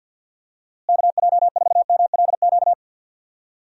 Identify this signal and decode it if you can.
Morse code, heard as a keyed tone, about 52 wpm, 700 Hz. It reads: KJ4MLQ
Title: USA Call Signs - 6 Characters - Repeated 3x - 50wpm